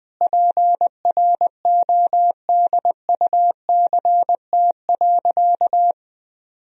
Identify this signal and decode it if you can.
Morse code, heard as a keyed tone, about 20 wpm, 700 Hz. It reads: PRODUCT.
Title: Sentences from Top 1000 Words - Review Part 2 - 20wpm